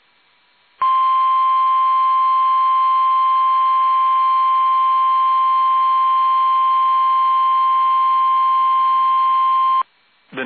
Emergency Alert System (SAME) tones (the 1050 Hz attention tone). It is heard from 0.8 to 9.9 s.
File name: eas / KUSO / KUSO